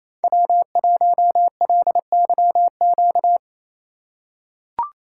Morse code, heard as a keyed tone, about 28 wpm, 700 Hz. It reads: W1LYQ
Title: USA Call Signs - 3x Character Spacing - 28wpm